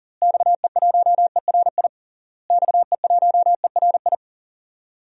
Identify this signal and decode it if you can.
Morse code, heard as a keyed tone, about 40 wpm, 700 Hz. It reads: XE1ERI XE1ERI
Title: DXCC-100 Call Signs - 6 Characters - Repeated 3x - 40wpm